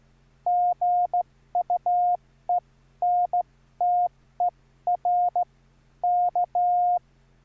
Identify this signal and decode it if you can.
Morse code, heard as a keyed tone, about 14 wpm, spaced as for 12 wpm, 710 Hz. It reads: GUENTER K
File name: QSO06